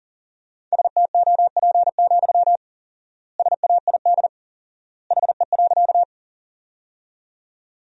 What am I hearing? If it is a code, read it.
Morse code, heard as a keyed tone, about 40 wpm, 700 Hz. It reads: STOP, SAID HE.
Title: 4x Word-Spacing - Sentences from Top 300 Words - 40wpm